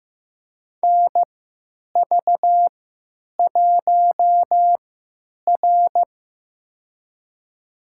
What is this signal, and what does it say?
Morse code, heard as a keyed tone, about 15 wpm, spaced as for 9 wpm, 700 Hz. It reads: NV1R
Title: USA Call Signs - 4 Characters - 3x Character Spacing - 15wpm